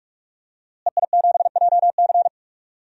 Morse code, heard as a keyed tone, about 46 wpm, 700 Hz. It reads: EI7JC
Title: DXCC-100 Call Signs - 5 Characters - Repeated 2x - 45wpm